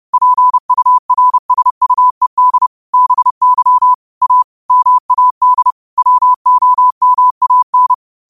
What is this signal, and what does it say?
Morse code, heard as a keyed tone, about 30 wpm, 1000 Hz. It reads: PURSUED BY A MAD WOMAN
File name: qotdfiles / 30wpm